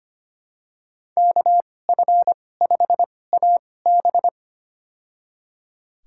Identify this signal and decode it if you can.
Morse code, heard as a keyed tone, about 25 wpm, spaced as for 18 wpm, 700 Hz. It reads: KF5AB
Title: USA Call Signs - 2x Character Spacing - 25wpm